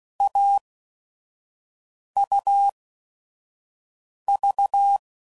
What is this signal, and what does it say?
Morse code, heard as a keyed tone, about 16 wpm, spaced as for 5 wpm, 800 Hz. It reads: AUV
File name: Combinations / Difficult Combos / AUV AUV4